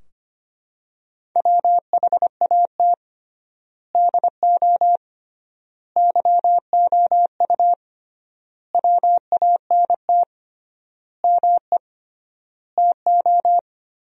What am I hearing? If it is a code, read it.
Morse code, heard as a keyed tone, about 25 wpm, 700 Hz. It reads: WHAT DO YOU WANT ME TO
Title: 3x Word-Spacing - Sentences from Top 200 Words - 25wpm